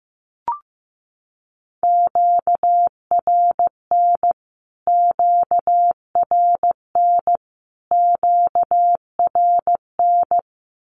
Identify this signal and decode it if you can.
Morse code, heard as a keyed tone, about 15 wpm, 700 Hz. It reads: QRN QRN QRN
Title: Q-Codes Repeated 3x 15wpm